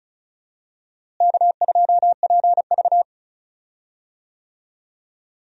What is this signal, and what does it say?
Morse code, heard as a keyed tone, about 36 wpm, 700 Hz. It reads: K2PV
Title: USA Call Signs - 4 Characters - Repeated 3x - 35wpm